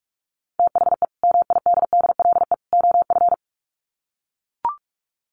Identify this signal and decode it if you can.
Morse code, heard as a keyed tone, about 44 wpm, 700 Hz. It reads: THE MIDDLE OF
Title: Sets of 3 Words 45wpm